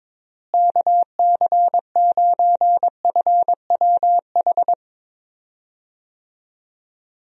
Morse code, heard as a keyed tone, about 22 wpm, 700 Hz. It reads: KC9FWH
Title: USA Call Signs - 6 Characters - Repeated 2x - 22wpm